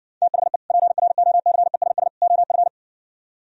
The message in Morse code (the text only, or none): THE PROCESS OF